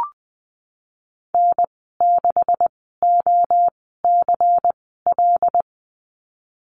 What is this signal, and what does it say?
Morse code, heard as a keyed tone, about 19 wpm, spaced as for 14 wpm, 700 Hz. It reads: N6OCL
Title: 154 Lesson 25g - Letter p - Call Signs 20wpm